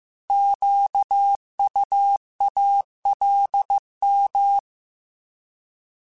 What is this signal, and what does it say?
Morse code, heard as a keyed tone, about 15 wpm, 800 Hz. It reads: QUALM
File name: Alphabetical / Alphabetical mp3s / Qs @13